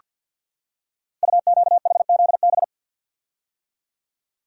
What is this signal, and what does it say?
Morse code, heard as a keyed tone, about 47 wpm, 700 Hz. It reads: UQHZB
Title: Code Groups - 5 Letters - 50wpm